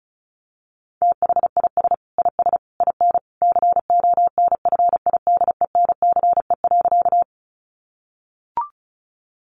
Morse code, heard as a keyed tone, about 34 wpm, 700 Hz. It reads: THIS IS IN CONFIDENCE.
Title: Sentences from Top 1500 Words 35wpm